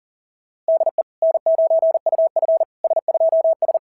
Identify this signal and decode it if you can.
Morse code, heard as a keyed tone, about 41 wpm, 690 Hz. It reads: DE N9UF S2S
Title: SOTA QSO Practice 40wpm